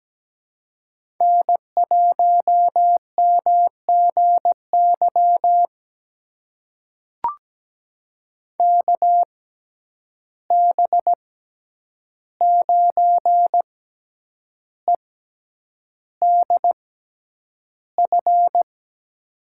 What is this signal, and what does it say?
Morse code, heard as a keyed tone, about 17 wpm, spaced as for 8 wpm, 700 Hz. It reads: N1MGY KB9EDF
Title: USA Call Signs - 6x Character Spacing - 17wpm